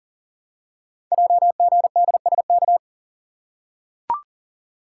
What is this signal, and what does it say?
Morse code, heard as a keyed tone, about 43 wpm, 700 Hz. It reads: JGDSK
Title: Code Groups - 5 Letter:Numbers - 40wpm